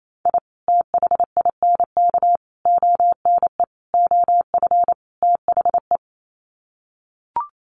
Morse code, heard as a keyed tone, about 28 wpm, 700 Hz. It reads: I THINK ONE OF THE
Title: Sets of 5 Words 28wpm